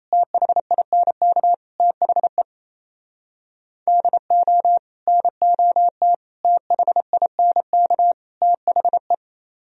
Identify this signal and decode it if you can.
Morse code, heard as a keyed tone, about 32 wpm, 700 Hz. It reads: THINK THE DO NOT THINK THE
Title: Speed-Racing - Sets of 4 Words 28wpm